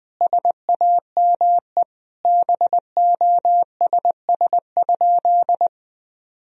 Morse code, heard as a keyed tone, about 21 wpm, 700 Hz. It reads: SAME BOSS?
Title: Sentences from Top 1000 Words - Review Part 4 - 20wpm